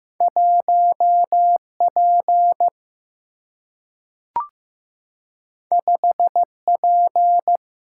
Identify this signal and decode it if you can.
Morse code, heard as a keyed tone, about 15 wpm, 700 Hz. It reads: 1 P 5 P
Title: Single-Letter Single-Number 15wpm